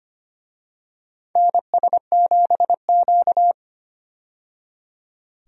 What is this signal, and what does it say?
Morse code, heard as a keyed tone, about 25 wpm, 700 Hz. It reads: NS7Q
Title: USA Call Signs - 4 Characters - 25wpm